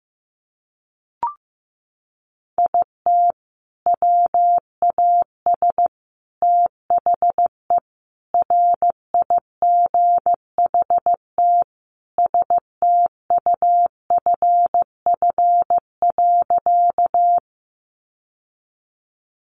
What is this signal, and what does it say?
Morse code, heard as a keyed tone, about 15 wpm, 700 Hz. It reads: IT WAS THE RIGHT STUFF.
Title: Sentences from Top 1400 Words 15wpm